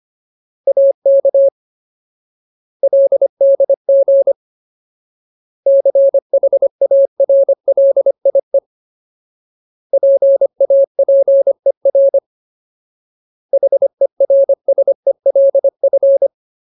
Morse code, heard as a keyed tone, about 25 wpm, 550 Hz. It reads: AK LDG CHARLIE PAPER HERSELF
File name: Ultimate Word List / 25WPM Ultimate